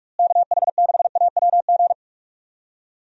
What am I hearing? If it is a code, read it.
Morse code, heard as a keyed tone, about 43 wpm, 700 Hz. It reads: KH6AWG
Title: DXCC-100 Call Signs - 6 Characters - Repeated 2x - 45wpm